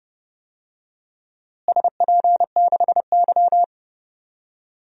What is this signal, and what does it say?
Morse code, heard as a keyed tone, about 32 wpm, 700 Hz. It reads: SP6Y
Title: DXCC-100 Call Signs - 4 Characters - Repeated 3x - 30wpm